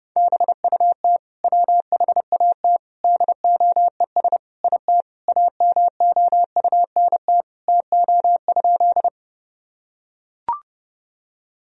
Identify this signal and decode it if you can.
Morse code, heard as a keyed tone, about 30 wpm, 700 Hz. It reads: BUT WHAT DOES IT AMOUNT TO?
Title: Sentences from Top 1000 Words - Review Part 4 - 30wpm